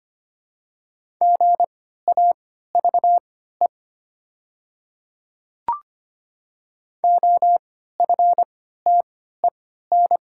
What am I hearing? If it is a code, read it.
Morse code, heard as a keyed tone, about 25 wpm, spaced as for 14 wpm, 690 Hz. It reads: GAVE OFTEN
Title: 3x Character Spacing - Top 500 Words - 25wpm